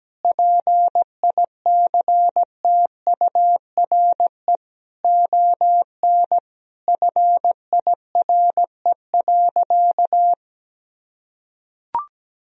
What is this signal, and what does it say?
Morse code, heard as a keyed tone, about 17 wpm, 700 Hz. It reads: PICTURE ON FIRE.
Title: Sentences from Top 1000 Words - Review Part 1 - 17wpm